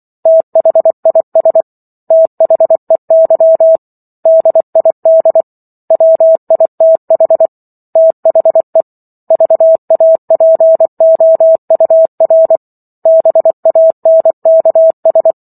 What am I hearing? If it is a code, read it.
Morse code, heard as a keyed tone, about 24 wpm, 650 Hz. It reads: THIS THEY DID WITH THE VAPOUR BANKS